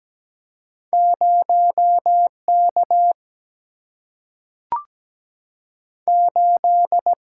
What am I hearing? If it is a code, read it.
Morse code, heard as a keyed tone, about 17 wpm, spaced as for 3 wpm, 700 Hz. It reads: 0 K8
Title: Single-Letter Single-Number 17wpm